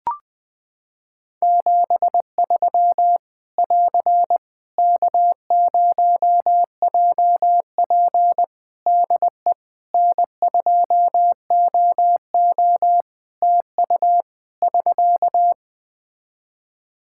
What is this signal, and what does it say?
Morse code, heard as a keyed tone, about 20 wpm, 700 Hz. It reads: 73 <AR> K0JP DE N2OO TU <SK>
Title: Practice QSO - Exchange 4 - Ending - 20wpm